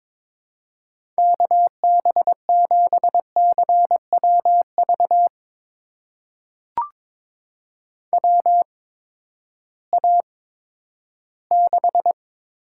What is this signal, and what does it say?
Morse code, heard as a keyed tone, about 22 wpm, spaced as for 12 wpm, 700 Hz. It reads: KB7CWV WA6
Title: USA Call Signs - 6 Characters - 8x Character Spacing - 22wpm